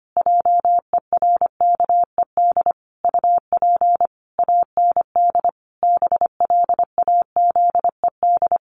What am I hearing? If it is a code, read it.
Morse code, heard as a keyed tone, about 25 wpm, 700 Hz. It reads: JERKED UP AND BLAZED
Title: A Sound Of Thunder 25wpm